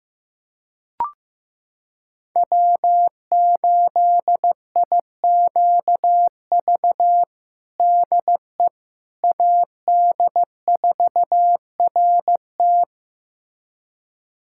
Morse code, heard as a keyed tone, about 16 wpm, 700 Hz. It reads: W8IQV DE AD4RT
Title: USA Callsign DE Callsign 15wpm